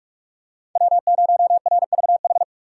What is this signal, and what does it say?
Morse code, heard as a keyed tone, about 46 wpm, 700 Hz. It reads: W0RVH
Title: USA Call Signs - 5 Characters - 3x Character Spacing - 45wpm